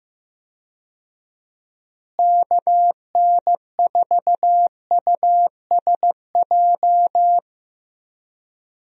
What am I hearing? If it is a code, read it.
Morse code, heard as a keyed tone, about 15 wpm, 700 Hz. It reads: KN4USJ
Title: USA Call Signs - 6 Characters - Repeated 3x - 15wpm